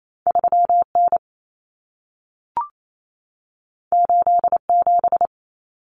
Morse code, heard as a keyed tone, about 28 wpm, 700 Hz. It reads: 3N 87
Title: Common Cut Numbers - 2-Digit 28wpm